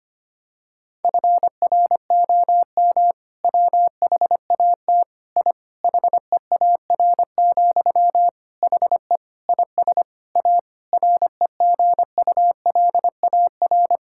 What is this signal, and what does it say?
Morse code, heard as a keyed tone, about 25 wpm, 700 Hz. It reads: FROM WHAT I HEAR, HE IS A REGULAR
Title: Sentences from Top 1000 Words - Review Part 6 - 25wpm